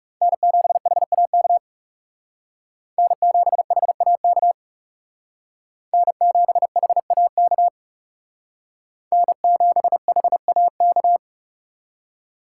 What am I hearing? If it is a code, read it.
Morse code, heard as a keyed tone, about 36 wpm, 700 Hz. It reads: N7HAK N7HAK N7HAK N7HAK
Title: Speed Racing - USA 1x3 Call Signs 30wpm